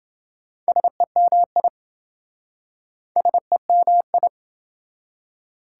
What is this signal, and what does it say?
Morse code, heard as a keyed tone, about 26 wpm, 700 Hz. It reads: SEMI SEMI